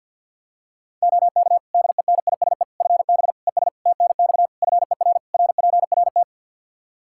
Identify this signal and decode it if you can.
Morse code, heard as a keyed tone, about 50 wpm, 700 Hz. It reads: OK DENISE FB ES TNX FER RPRT
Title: Practice QSO - Exchange 2 - Info - 50wpm